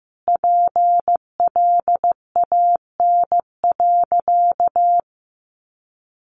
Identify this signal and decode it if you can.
Morse code, heard as a keyed tone, about 14 wpm, 700 Hz. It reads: PLAN.
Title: Sentences from Top 500 Words 15wpm